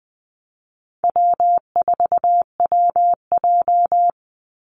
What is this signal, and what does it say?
Morse code, heard as a keyed tone, about 20 wpm, 700 Hz. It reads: W4WJ
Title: Easy Call Signs - 20wpm